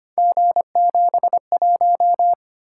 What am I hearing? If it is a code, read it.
Morse code, heard as a keyed tone, about 25 wpm, 700 Hz. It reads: G71
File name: Three Letter-Number Combinations - 25wpm